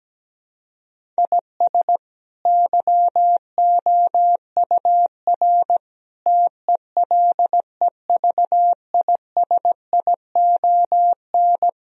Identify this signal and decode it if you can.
Morse code, heard as a keyed tone, about 17 wpm, 700 Hz. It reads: IS YOUR TELEVISION
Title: Sentences from Top 1000 Words - Review Part 4 - 17wpm